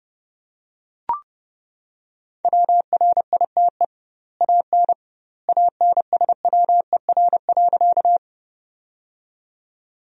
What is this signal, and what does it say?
Morse code, heard as a keyed tone, about 30 wpm, 700 Hz. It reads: WRITE AN ANSWER.
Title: 2x Word-Spacing - Sentences from Top 300 Words - 30wpm